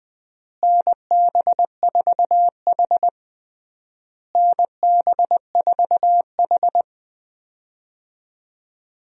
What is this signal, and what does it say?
Morse code, heard as a keyed tone, about 20 wpm, 700 Hz. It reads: NB4H NB4H
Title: USA Call Signs - 4 Characters - Repeated 3x - 20wpm